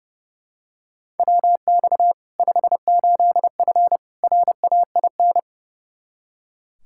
Morse code, heard as a keyed tone, about 30 wpm, 700 Hz. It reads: WX 58F RAIN
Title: Practice QSO - Exchange 2 - Info - 30wpm